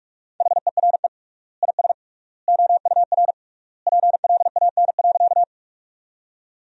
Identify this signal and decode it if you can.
Morse code, heard as a keyed tone, about 46 wpm, 700 Hz. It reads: HERE IS OUR PLAN.
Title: 3x Word-Spacing - Sentences from Top 500 Words - 45wpm